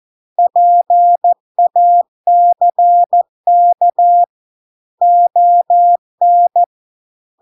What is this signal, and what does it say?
Morse code, heard as a keyed tone, about 14 wpm, 700 Hz. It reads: PACK ON